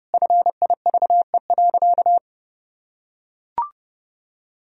Morse code, heard as a keyed tone, about 32 wpm, 700 Hz. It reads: FIVE.
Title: Sentences from Top 1000 Words - Review Part 2 - 30wpm